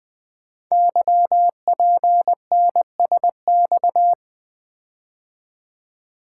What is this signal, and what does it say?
Morse code, heard as a keyed tone, about 21 wpm, 700 Hz. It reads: YPNSX
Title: Code Groups - 5 Letter:Numbers - 20wpm